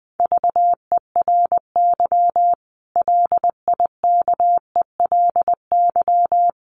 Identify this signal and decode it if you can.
Morse code, heard as a keyed tone, about 20 wpm, 700 Hz. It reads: VERY LIKELY
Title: Sentences from Top 600 Words 20wpm